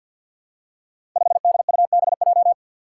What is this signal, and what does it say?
Morse code, heard as a keyed tone, about 50 wpm, 700 Hz. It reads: 5DUBJ
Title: Five Challenging Letter-Number Combinations 50wpm